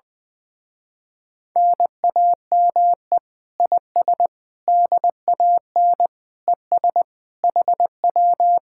Morse code, heard as a keyed tone, about 20 wpm, 700 Hz. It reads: NAME IS DAN ES HW
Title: Speed-Racing Common QSO Phrases - 20wpm